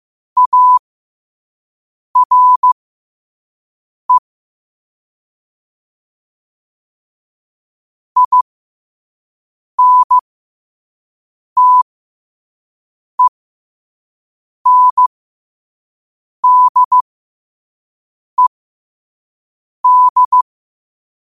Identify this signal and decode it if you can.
Morse code, heard as a keyed tone, about 15 wpm, spaced as for 5 wpm, 1000 Hz. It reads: ARE INTENDED
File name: qotdfiles / 5wpm